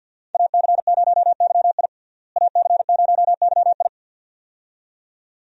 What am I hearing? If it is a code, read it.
Morse code, heard as a keyed tone, about 53 wpm, 700 Hz. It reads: AC0YI AC0YI
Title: USA Call Signs - 5 Characters - Repeated 2x - 50wpm